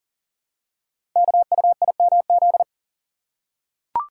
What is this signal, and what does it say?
Morse code, heard as a keyed tone, about 42 wpm, 690 Hz. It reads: KUIMZ
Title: Code Groups - 5 Letters - 40wpm